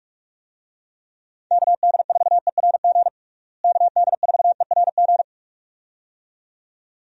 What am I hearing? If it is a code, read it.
Morse code, heard as a keyed tone, about 43 wpm, 700 Hz. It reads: KD4ERG KD4ERG
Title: USA Call Signs - 6 Characters - Repeated 2x - 45wpm